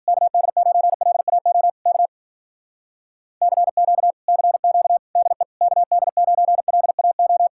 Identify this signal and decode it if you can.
Morse code, heard as a keyed tone, about 53 wpm, 700 Hz. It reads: KD9LAO K CQ CQ DE KD9LAO